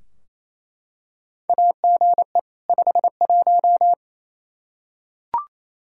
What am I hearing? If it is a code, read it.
Morse code, heard as a keyed tone, about 27 wpm, 700 Hz. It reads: AGE 51
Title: Practice QSO - Exchange 3 - More Info - 28wpm